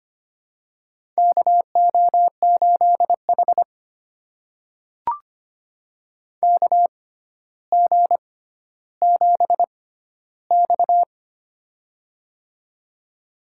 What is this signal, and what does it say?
Morse code, heard as a keyed tone, about 25 wpm, spaced as for 13 wpm, 700 Hz. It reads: KO8H KG7X
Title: USA Call Signs - 4 Characters - 6x Character Spacing - 25wpm